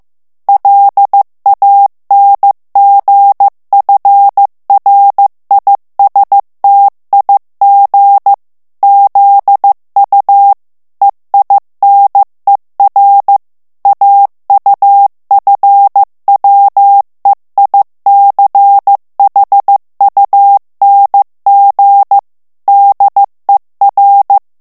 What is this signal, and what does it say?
Morse code, heard as a keyed tone, about 15 wpm, 800 Hz. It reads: LANGFRISTIG ZU EINER AUFWEICHUNG DER
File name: morse mp3 / ALLEMAND texte clair 1 vitesse 14 mots minute